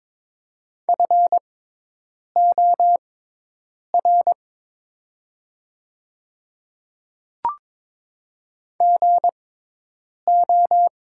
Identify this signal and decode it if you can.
Morse code, heard as a keyed tone, about 22 wpm, spaced as for 8 wpm, 700 Hz. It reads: FOR GO
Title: 6x Character Spacing - Top 100 Words - 22wpm